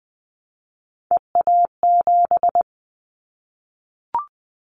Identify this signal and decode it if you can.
Morse code, heard as a keyed tone, about 19 wpm, 700 Hz. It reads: EA7
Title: Callsign Prefixes for Top 25 DXCC - Repeated 3x - 20wpm